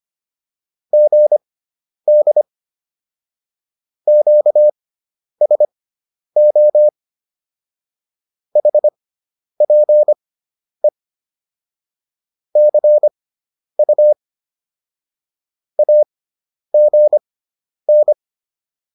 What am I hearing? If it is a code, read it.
Morse code, heard as a keyed tone, about 25 wpm, spaced as for 10 wpm, 600 Hz. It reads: GD QSO HPE CU AGN
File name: QSO306 10